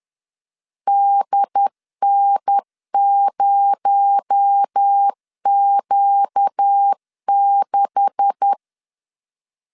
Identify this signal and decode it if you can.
Morse code, heard as a keyed tone, about 10 wpm, 800 Hz. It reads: DN0Q6
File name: LEZIONI / 10 SIGNS / 10A SLASH 10wpm 2808